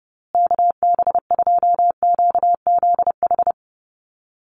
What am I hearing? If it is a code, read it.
Morse code, heard as a keyed tone, about 30 wpm, 700 Hz. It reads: KB2QZH
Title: USA Call Signs Repeated 2x 30wpm